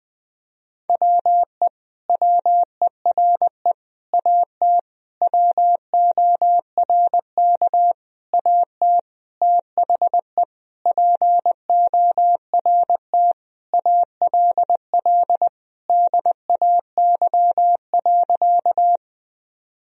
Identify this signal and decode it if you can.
Morse code, heard as a keyed tone, about 21 wpm, 700 Hz. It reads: WE WERE AT WORK AT THE PORT ALL DAY.
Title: Sentences from Top 1000 Words - Review Part 1 - 20wpm